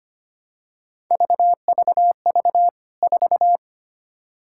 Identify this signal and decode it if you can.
Morse code, heard as a keyed tone, about 25 wpm, 700 Hz. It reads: VVV 4
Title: K1USN SST QSO Practice 25wpm